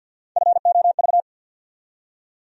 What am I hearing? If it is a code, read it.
Morse code, heard as a keyed tone, about 53 wpm, 700 Hz. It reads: FOV
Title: USA Call Sign Postfixes - 3 Letters 50wpm